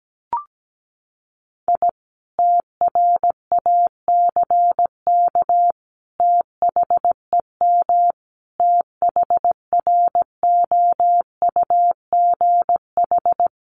Morse code, heard as a keyed tone, about 17 wpm, 700 Hz. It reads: I TRACK THEM THROUGH
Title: A Sound Of Thunder 17wpm